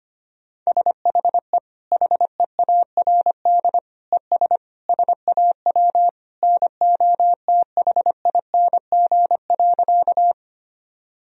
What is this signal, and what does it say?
Morse code, heard as a keyed tone, about 25 wpm, 700 Hz. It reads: SHE HEARD ES SAW NOTHING.
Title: Sentences from Top 500 Words and Abbreviations 25wpm